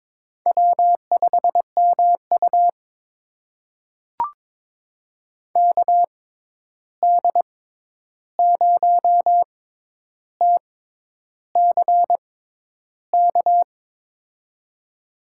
Morse code, heard as a keyed tone, about 22 wpm, spaced as for 10 wpm, 700 Hz. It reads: W5MU KD0TCK
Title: USA Call Signs - 6x Character Spacing - 22wpm